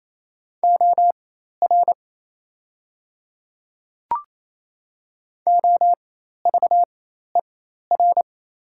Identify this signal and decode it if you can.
Morse code, heard as a keyed tone, about 28 wpm, spaced as for 13 wpm, 690 Hz. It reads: OR OVER
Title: 4x Character Spacing - Top 200 Words - 28wpm